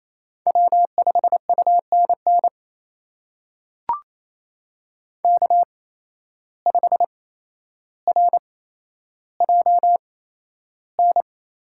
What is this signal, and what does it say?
Morse code, heard as a keyed tone, about 27 wpm, spaced as for 12 wpm, 700 Hz. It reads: W5UNN K5RJN
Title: USA Call Signs - 5 Characters - 8x Character Spacing - 28wpm